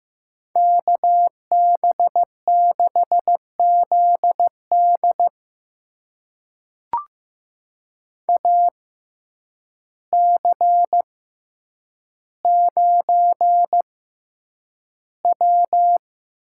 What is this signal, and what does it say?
Morse code, heard as a keyed tone, about 15 wpm, spaced as for 8 wpm, 700 Hz. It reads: KB6ZD AC9W
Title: USA Call Signs - 6x Character Spacing - 15wpm